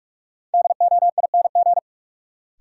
Morse code, heard as a keyed tone, about 43 wpm, 700 Hz. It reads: DOING
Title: Sentences from Top 1400 Words - New Word Drill Alt - 45wpm